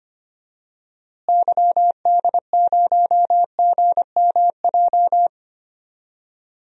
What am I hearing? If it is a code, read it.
Morse code, heard as a keyed tone, about 25 wpm, 700 Hz. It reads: YD0GMJ
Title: DXCC-100 Call Signs - 6 Characters - Repeated 2x - 25wpm